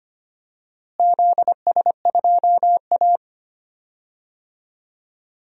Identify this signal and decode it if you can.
Morse code, heard as a keyed tone, about 25 wpm, 700 Hz. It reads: ZS2A
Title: DXCC-100 Call Signs - 4 Characters - Repeated 2x - 25wpm